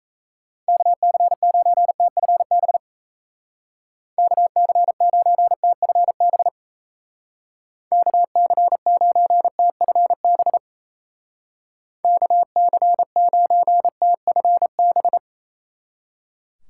Morse code, heard as a keyed tone, about 34 wpm, 700 Hz. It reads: KC9TFB KC9TFB KC9TFB KC9TFB
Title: Speed Racing - USA Call Signs 28wpm